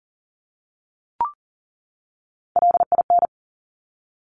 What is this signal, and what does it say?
Morse code, heard as a keyed tone, about 40 wpm, 700 Hz. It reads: LIN
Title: Top Three Letter Combinations in Words 40wpm